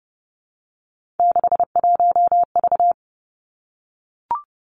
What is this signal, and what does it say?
Morse code, heard as a keyed tone, about 32 wpm, 700 Hz. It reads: 61V
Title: Three Challenging Letter-Number Combinations 30wpm